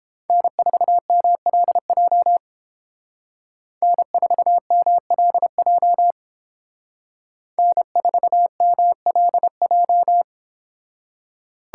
Morse code, heard as a keyed tone, about 28 wpm, 700 Hz. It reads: N4MLJ N4MLJ N4MLJ